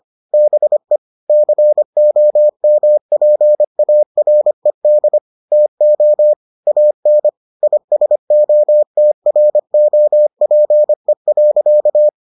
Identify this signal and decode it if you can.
Morse code, heard as a keyed tone, about 25 wpm, 600 Hz. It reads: BE COMPARED TO AN ISOTROPE.